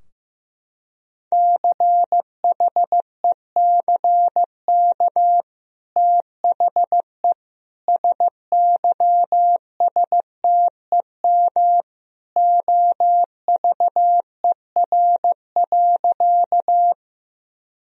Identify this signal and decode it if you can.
Morse code, heard as a keyed tone, about 16 wpm, 700 Hz. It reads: CHECK THE SYSTEM OVER.
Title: Sentences from Top 1000 Words - Review Part 3 - 15wpm